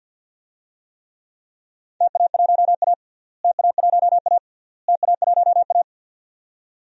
Morse code, heard as a keyed tone, about 47 wpm, 700 Hz. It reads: TA1A TA1A TA1A
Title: DXCC-100 Call Signs - 4 Characters - Repeated 3x - 50wpm